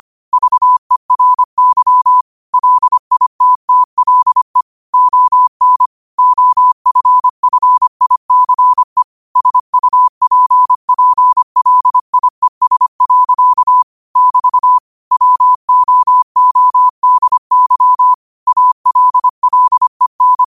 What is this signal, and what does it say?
Morse code, heard as a keyed tone, about 25 wpm, 1000 Hz. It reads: VERY LITTLE ON OFFICE SUPPLIES. <BT> WOODY ALLEN
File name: qotdfiles / 25wpm